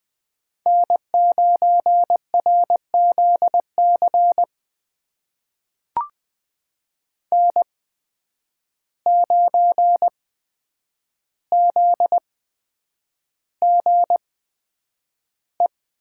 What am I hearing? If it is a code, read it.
Morse code, heard as a keyed tone, about 20 wpm, spaced as for 9 wpm, 700 Hz. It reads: N9RZC N9ZGE
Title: USA Call Signs - 8x Character Spacing - 20wpm